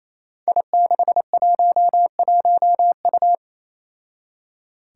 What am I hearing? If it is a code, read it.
Morse code, heard as a keyed tone, about 27 wpm, 700 Hz. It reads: I611U
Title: Five Challenging Letter-Number Combinations 28wpm